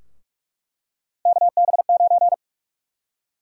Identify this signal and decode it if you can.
Morse code, heard as a keyed tone, about 46 wpm, 700 Hz. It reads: KB9
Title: USA Call Sign Prefixes - Repeated 2x - 45wpm